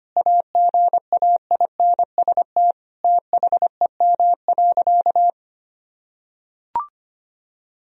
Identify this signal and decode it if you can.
Morse code, heard as a keyed tone, about 25 wpm, 700 Hz. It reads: AGAINST THEM.
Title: Sentences from Top 1000 Words - Review Part 4 - 25wpm